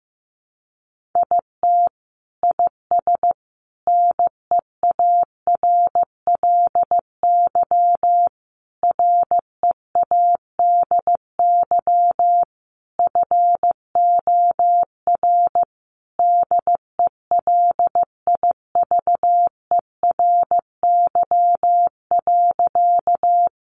Morse code, heard as a keyed tone, about 15 wpm, 700 Hz. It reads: IT IS NEARLY READY FOR DELIVERY.
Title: Sentences from Top 1500 Words 15wpm